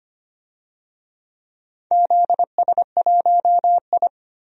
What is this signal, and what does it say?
Morse code, heard as a keyed tone, about 25 wpm, 700 Hz. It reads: ZS1I
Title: DXCC-100 Call Signs - 4 Characters - Repeated 2x - 25wpm